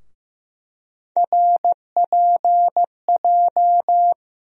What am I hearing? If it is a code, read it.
Morse code, heard as a keyed tone, about 15 wpm, 700 Hz. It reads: RPJ
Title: USA Call Sign Postfixes - 3 Letters 15wpm